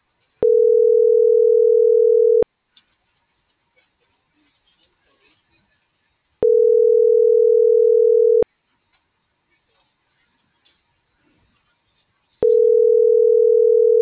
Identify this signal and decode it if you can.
Ringback tone (North America), 440 + 480 Hz.